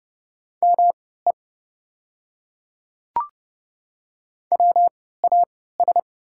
Morse code, heard as a keyed tone, about 32 wpm, spaced as for 17 wpm, 700 Hz. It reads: ME WAS
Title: 3x Character Spacing - Top 200 Words - 30wpm